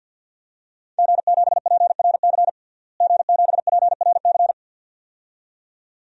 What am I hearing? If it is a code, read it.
Morse code, heard as a keyed tone, about 52 wpm, 700 Hz. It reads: G7PRC G7PRC
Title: DXCC-100 Call Signs - 5 Characters - Repeated 2x - 50wpm